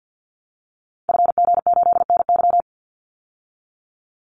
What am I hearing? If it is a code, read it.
Morse code, heard as a keyed tone, about 50 wpm, 700 Hz. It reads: FG8NY
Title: Call Signs Repeated 2x - 50wpm